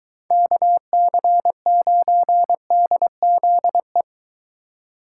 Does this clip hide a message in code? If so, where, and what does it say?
Yes: Morse code from 0.3 to 4.0 s, heard as a keyed tone, about 23 wpm, 700 Hz. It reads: KC9DZE